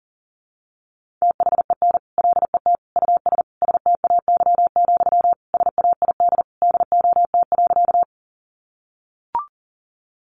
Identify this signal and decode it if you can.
Morse code, heard as a keyed tone, about 40 wpm, 700 Hz. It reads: THEN LET US STAY, SAID DOT.
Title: Sentences from Top 1500 Words 40wpm